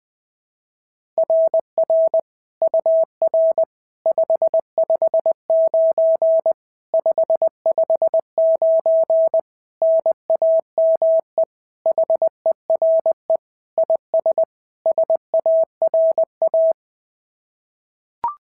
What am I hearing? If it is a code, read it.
Morse code, heard as a keyed tone, about 19 wpm, 650 Hz. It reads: RR UR 559 559 NAME HERE IS SARA
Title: SOTA QSO Practice 20wpm